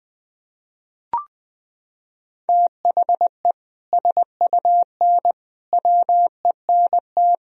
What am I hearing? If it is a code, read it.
Morse code, heard as a keyed tone, about 20 wpm, 700 Hz. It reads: THE SUN WENT
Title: Sentences from Top 1000 Words - Review Part 1 - 20wpm